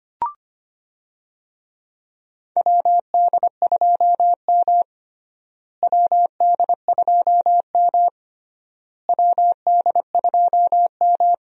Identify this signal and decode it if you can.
Morse code, heard as a keyed tone, about 25 wpm, 700 Hz. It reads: WD2M WD2M WD2M
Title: USA Call Signs - 4 Characters - Repeated 3x - 25wpm